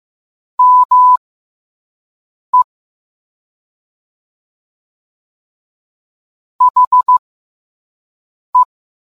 Morse code, heard as a keyed tone, about 15 wpm, spaced as for 5 wpm, 1000 Hz. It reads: ME HE